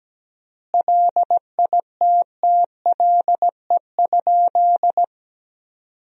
Morse code, heard as a keyed tone, about 17 wpm, 700 Hz. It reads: LITTLE?
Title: 3x Word-Spacing - Sentences from Top 500 Words - 17wpm